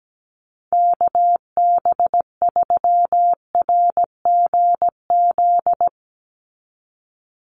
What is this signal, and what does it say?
Morse code, heard as a keyed tone, about 17 wpm, 700 Hz. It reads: KB3RGZ
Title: USA Call Signs 17wpm